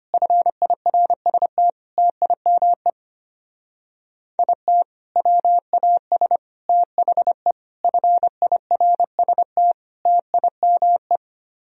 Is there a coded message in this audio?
Yes: Morse code from 0.1 to 11.2 s, heard as a keyed tone, about 32 wpm, 700 Hz. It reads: FIRST TIME IT WAS THE FIRST TIME